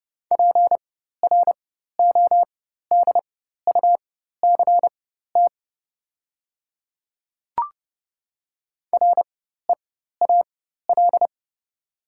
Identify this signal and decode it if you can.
Morse code, heard as a keyed tone, about 31 wpm, spaced as for 14 wpm, 690 Hz. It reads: PRODUCT REAL
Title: 4x Character Spacing - Top 500 Words - 30wpm